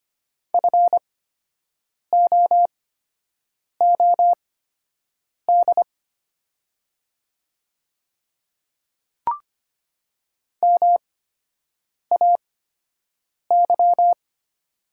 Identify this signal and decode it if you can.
Morse code, heard as a keyed tone, about 25 wpm, spaced as for 7 wpm, 700 Hz. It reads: FOOD MAY
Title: 8x Character Spacing - Top 300 Words - 25wpm